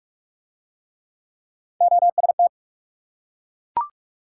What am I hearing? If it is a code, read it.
Morse code, heard as a keyed tone, about 41 wpm, 700 Hz. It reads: OST